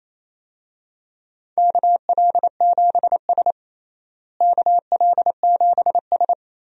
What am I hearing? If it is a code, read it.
Morse code, heard as a keyed tone, about 28 wpm, 700 Hz. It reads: KL7S KL7S
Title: DXCC-100 Call Signs - 4 Characters - Repeated 3x - 28wpm